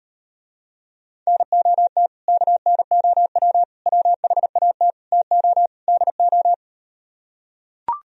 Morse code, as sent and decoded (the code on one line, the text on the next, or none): -. --- - / -.- -. --- .-- / .-- .... .- - / - --- / -.. ---
NOT KNOW WHAT TO DO